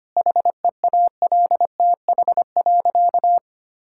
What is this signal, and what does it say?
Morse code, heard as a keyed tone, about 25 wpm, 700 Hz. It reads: HEALTH.
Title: Sentences from Top 1000 Words - Review Part 6 - 25wpm